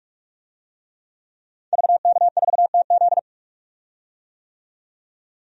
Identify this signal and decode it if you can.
Morse code, heard as a keyed tone, about 41 wpm, 700 Hz. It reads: VK4TZ
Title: DXCC-100 Call Signs - 5 Characters - 45wpm